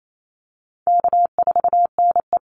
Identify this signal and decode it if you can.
Morse code, heard as a keyed tone, about 28 wpm, 700 Hz. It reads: K4NE
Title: Easy Call Signs - 28wpm